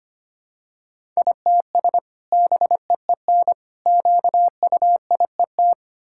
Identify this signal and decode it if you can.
Morse code, heard as a keyed tone, about 25 wpm, 700 Hz. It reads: ITS BEEN QUIET
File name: practice / 25wpm / Sentences with 2-Word Phrases - 25wpm